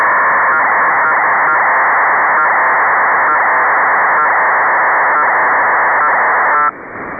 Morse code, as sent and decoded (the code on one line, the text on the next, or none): none